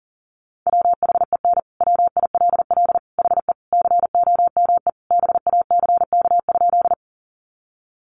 Morse code, heard as a keyed tone, about 37 wpm, 700 Hz. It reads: WHEN WILL HE COME BACK?
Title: Sentences from Top 200 Words 40wpm